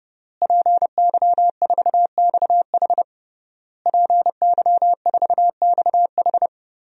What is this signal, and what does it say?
Morse code, heard as a keyed tone, about 32 wpm, 700 Hz. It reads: PY4XH PY4XH
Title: DXCC-100 Call Signs - 5 Characters - Repeated 3x - 30wpm